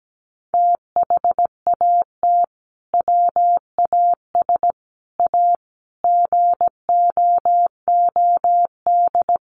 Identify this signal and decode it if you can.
Morse code, heard as a keyed tone, about 17 wpm, 700 Hz. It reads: THAT WAS A GOOD
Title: Sentences from Top 200 Words 17wpm